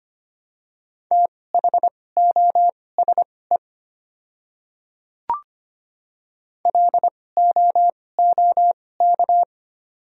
Morse code, heard as a keyed tone, about 25 wpm, spaced as for 18 wpm, 700 Hz. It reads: THOSE LOOK
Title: 2x Character Spacing - Top 300 Words - 25wpm